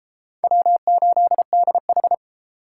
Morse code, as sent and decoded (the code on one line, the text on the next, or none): .-- ---.. -.. ....
W8DH